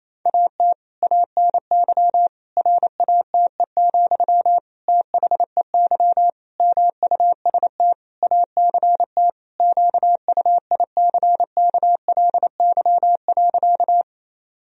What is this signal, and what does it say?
Morse code, heard as a keyed tone, about 28 wpm, 700 Hz. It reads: AT ANY RATE, THEY MUST ACT QUICKLY.
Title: Sentences from Top 1000 Words - Review Part 4 - 28wpm